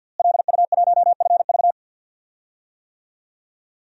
Morse code, heard as a keyed tone, about 50 wpm, 700 Hz. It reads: LU1FV
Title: DXCC-100 Call Signs - 5 Characters - Repeated 3x - 50wpm